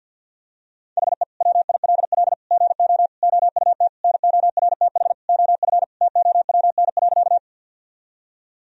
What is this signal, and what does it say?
Morse code, heard as a keyed tone, about 50 wpm, 700 Hz. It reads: HE WILL GO OUT NORTH OF TOWN.
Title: Sentences from Top 1000 Words - Review Part 1 - 50wpm